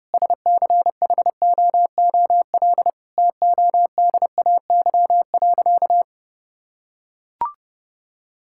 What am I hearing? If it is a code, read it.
Morse code, heard as a keyed tone, about 30 wpm, 700 Hz. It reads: SCHOOL TODAY.
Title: Sentences from Top 1000 Words - Review Part 5 - 30wpm